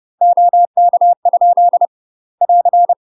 Morse code, heard as a keyed tone, about 30 wpm, 700 Hz. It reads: OK? <AR>
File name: UK MASTER QSOs TRAINING/30WPM